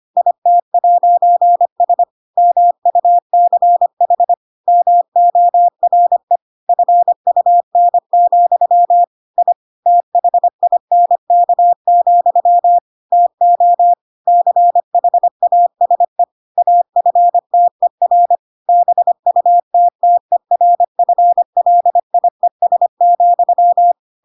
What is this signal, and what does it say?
Morse code, heard as a keyed tone, about 25 wpm, 700 Hz. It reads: IT'S MUCH MORE FUN, I THINK, TO CHASE AFTER BUTTERFLIES,
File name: Pinocchio / Pinocchio ch4 25 25 ik1hge